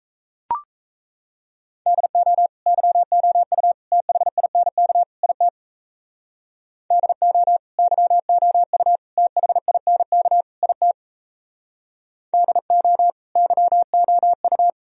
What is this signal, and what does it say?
Morse code, heard as a keyed tone, about 37 wpm, 700 Hz. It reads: DO YOU THINK IT DO YOU THINK IT DO YOU